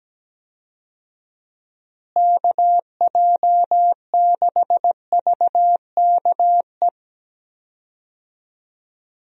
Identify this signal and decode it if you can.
Morse code, heard as a keyed tone, about 17 wpm, 700 Hz. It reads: KJ6VKE
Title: USA Call Signs - 6 Characters - Repeated 3x - 17wpm